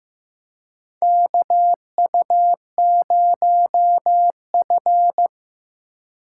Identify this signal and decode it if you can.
Morse code, heard as a keyed tone, about 15 wpm, 700 Hz. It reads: KU0F
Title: USA Call Signs - 2x Character Spacing - 15wpm